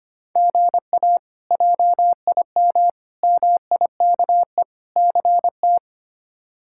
Morse code, heard as a keyed tone, about 25 wpm, 700 Hz. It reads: GA JIM MIKE CT
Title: K1USN SST QSO Practice - 25wpm